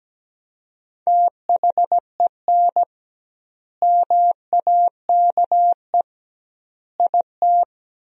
Morse code, heard as a keyed tone, about 17 wpm, 700 Hz. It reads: THEN MAKE IT
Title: 2x Word-Spacing - Sentences from Top 100 Words - 17wpm